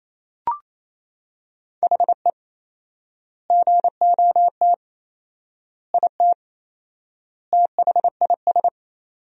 Morse code, heard as a keyed tone, about 28 wpm, 700 Hz. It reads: HE GOT IT THIS
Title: 4x Word-Spacing - Sentences from Top 400 Words - 28wpm